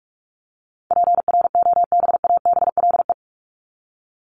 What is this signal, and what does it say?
Morse code, heard as a keyed tone, about 44 wpm, 700 Hz. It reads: PROBABLE
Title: LY Suffix - Top 100 Words with Base Words 45wpm